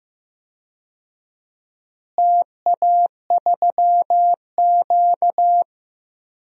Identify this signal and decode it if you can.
Morse code, heard as a keyed tone, about 15 wpm, 700 Hz. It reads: TA3Q
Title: DXCC-100 Call Signs - 4 Characters - Repeated 3x - 15wpm